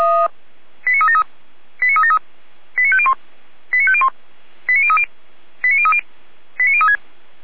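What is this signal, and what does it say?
Five-tone selective call (ZVEI), heard as quick runs of five tones, tones of about 80 ms. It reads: 89373 89373 89501 89501 89030 89030 89036
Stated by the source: ZVEI five-tone call 89042 received on SDR.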